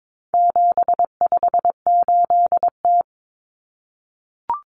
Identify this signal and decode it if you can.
Morse code, heard as a keyed tone, about 22 wpm, 700 Hz. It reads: 758T
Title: Common Cut Numbers - 4-Digit 22wpm